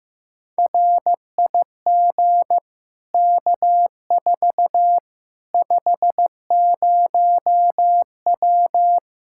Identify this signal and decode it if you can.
Morse code, heard as a keyed tone, about 15 wpm, 700 Hz. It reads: RIG K4 50W
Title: Practice QSO - Exchange 2 - Info - 15wpm